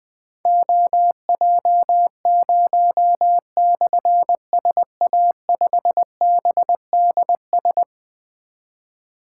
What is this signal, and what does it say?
Morse code, heard as a keyed tone, about 21 wpm, 700 Hz. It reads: OJ0/SA5BDS
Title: Difficult Call Signs 20wpm